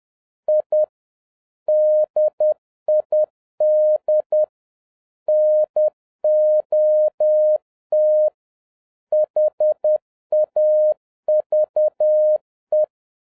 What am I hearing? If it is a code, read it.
Morse code, heard as a keyed tone, about 10 wpm, 610 Hz. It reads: I DID NOT HAVE
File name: 2011-07-mp3 / n41 EXCITING QRP CW OPERATION 10wpm